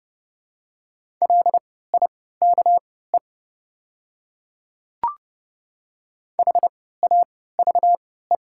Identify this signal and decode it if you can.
Morse code, heard as a keyed tone, about 31 wpm, spaced as for 17 wpm, 700 Hz. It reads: LIKE HAVE
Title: 3x Character Spacing - Top 100 Words - 30wpm